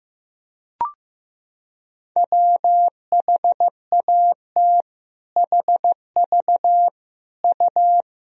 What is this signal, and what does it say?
Morse code, heard as a keyed tone, about 15 wpm, 700 Hz. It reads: WHAT HV U
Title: Sentences from Top 200 Words and Abbreviations 15wpm